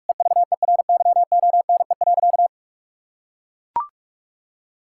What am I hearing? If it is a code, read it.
Morse code, heard as a keyed tone, about 42 wpm, 700 Hz. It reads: EVERYONE.
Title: Sentences from Top 1000 Words - Review Part 5 - 45wpm